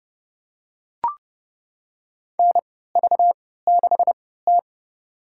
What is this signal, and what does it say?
Morse code, heard as a keyed tone, about 31 wpm, spaced as for 17 wpm, 700 Hz. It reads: NV6T
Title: USA Call Signs - 4 Characters - 3x Character Spacing - 30wpm